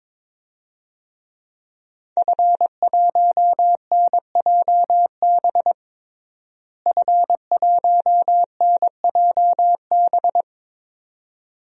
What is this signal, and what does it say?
Morse code, heard as a keyed tone, about 22 wpm, 700 Hz. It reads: F1NJB F1NJB
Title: DXCC-100 Call Signs - 5 Characters - Repeated 2x - 22wpm